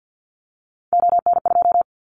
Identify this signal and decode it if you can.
Morse code, heard as a keyed tone, about 50 wpm, 700 Hz. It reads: ON2
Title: Callsign Prefixes for Top 100 DXCC - 50wpm